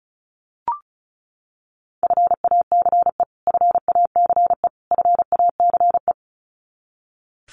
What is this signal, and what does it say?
Morse code, heard as a keyed tone, about 34 wpm, 700 Hz. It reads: FACE FACE FACE
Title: Top 300 Words Repeated 35wpm